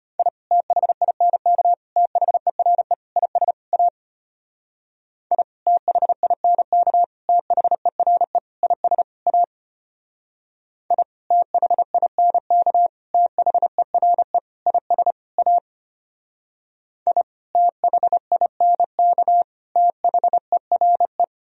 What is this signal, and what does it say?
Morse code, heard as a keyed tone, about 32 wpm, 700 Hz. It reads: I THINK THERE IS A I THINK THERE IS A I THINK THERE IS A I THINK THERE